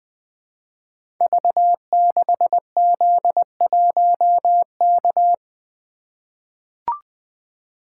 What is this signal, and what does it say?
Morse code, heard as a keyed tone, about 20 wpm, 700 Hz. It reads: V6Z1K
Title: Code Groups - 5 Letter:Numbers - 20wpm